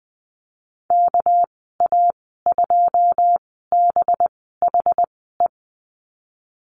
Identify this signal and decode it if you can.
Morse code, heard as a keyed tone, about 20 wpm, spaced as for 14 wpm, 700 Hz. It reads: KA2BHE
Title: 219 Lesson 35g - Number 0 - Call Signs 20wpm